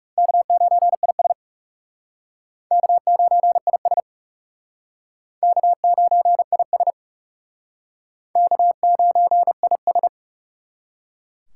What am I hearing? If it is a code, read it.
Morse code, heard as a keyed tone, about 40 wpm, 700 Hz. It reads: K9IS K9IS K9IS K9IS